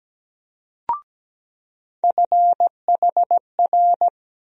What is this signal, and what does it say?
Morse code, heard as a keyed tone, about 17 wpm, 700 Hz. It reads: FHR
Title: USA Call Sign Postfixes - 3 Letters 17wpm